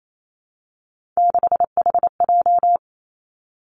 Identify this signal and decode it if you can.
Morse code, heard as a keyed tone, about 28 wpm, 700 Hz. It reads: 6HJ
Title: Three Challenging Letter-Number Combinations 28wpm